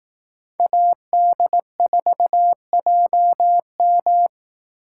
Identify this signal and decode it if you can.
Morse code, heard as a keyed tone, about 18 wpm, 700 Hz. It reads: AD4JM
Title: Speed Racing - USA 2x2 Call Signs 15wpm